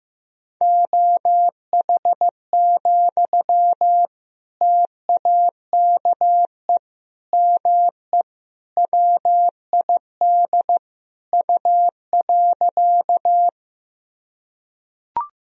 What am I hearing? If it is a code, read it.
Morse code, heard as a keyed tone, about 15 wpm, 700 Hz. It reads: OH, TAKE ME WID U.
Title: Sentences from Top 500 Words and Abbreviations 15wpm